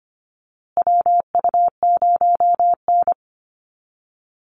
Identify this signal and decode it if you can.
Morse code, heard as a keyed tone, about 25 wpm, 700 Hz. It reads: WU0N
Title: USA Call Signs Repeated 2x 25wpm